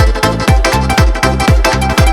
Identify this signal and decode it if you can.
There is no code.